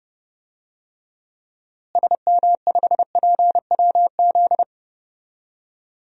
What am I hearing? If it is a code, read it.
Morse code, heard as a keyed tone, about 31 wpm, 700 Hz. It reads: SM5PWZ
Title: DXCC-100 Call Signs - 6 Characters - Repeated 3x - 30wpm